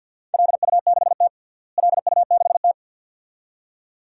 Morse code, heard as a keyed tone, about 52 wpm, 700 Hz. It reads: LU6T LU6T
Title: DXCC-100 Call Signs - 4 Characters - Repeated 2x - 50wpm